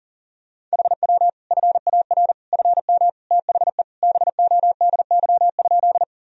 Morse code, heard as a keyed tone, about 40 wpm, 700 Hz. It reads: HW FAR FM THE BODY?
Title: Sentences from Top 400 Words and Abbreviations 40wpm